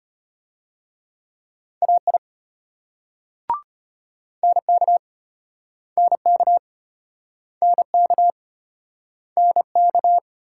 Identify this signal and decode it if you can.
Morse code, heard as a keyed tone, about 35 wpm, 700 Hz. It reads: AI NK NK NK NK